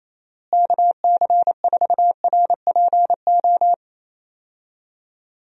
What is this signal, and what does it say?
Morse code, heard as a keyed tone, about 28 wpm, 700 Hz. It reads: KC4RPO
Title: USA Call Signs - 6 Characters - Repeated 3x - 28wpm